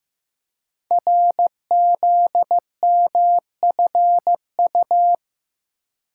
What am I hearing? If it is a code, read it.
Morse code, heard as a keyed tone, about 15 wpm, 700 Hz. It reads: RZMFU
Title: Code Groups - 5 Letter:Numbers - 15wpm